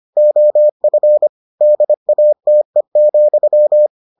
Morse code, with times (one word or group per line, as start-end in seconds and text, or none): 0.2-1.3 OF
1.6-3.9 DATE,